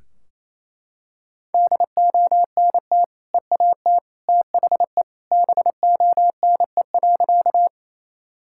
Morse code, heard as a keyed tone, about 28 wpm, 700 Hz. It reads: DONT EAT THE BONE.
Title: Sentences from Top 1600 Words 28wpm